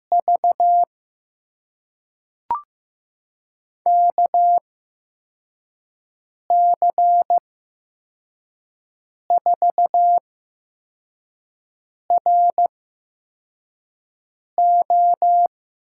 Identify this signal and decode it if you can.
Morse code, heard as a keyed tone, about 15 wpm, spaced as for 4 wpm, 700 Hz. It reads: V KC4RO
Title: USA Call Signs - 6 Characters - 8x Character Spacing - 15wpm